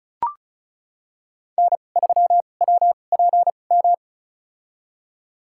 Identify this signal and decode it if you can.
Morse code, heard as a keyed tone, about 34 wpm, spaced as for 25 wpm, 700 Hz. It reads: N3WPM
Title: USA Call Signs - 5 Characters - 2x Character Spacing - 35wpm